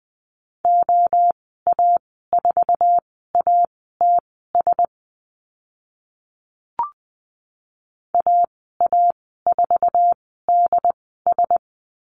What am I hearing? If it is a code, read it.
Morse code, heard as a keyed tone, about 20 wpm, spaced as for 14 wpm, 700 Hz. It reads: OA4ATS AA4DS
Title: 082 Lesson 14g - Number 2 - Call Signs 20wpm